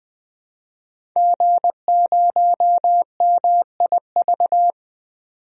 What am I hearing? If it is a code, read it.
Morse code, heard as a keyed tone, about 21 wpm, 700 Hz. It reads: G0MIV
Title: DXCC-100 Call Signs - 5 Characters - 20wpm